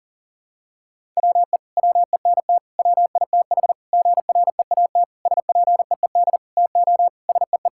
Morse code, heard as a keyed tone, about 40 wpm, 700 Hz. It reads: WE WENT WITH GREAT SPEED TO SEE
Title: Sentences from Top 1000 Words - Review Part 5 - 40wpm